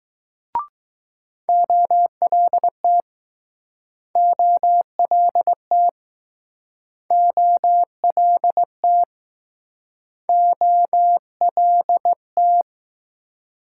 Morse code, heard as a keyed tone, about 19 wpm, 700 Hz. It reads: OLT OLT OLT OLT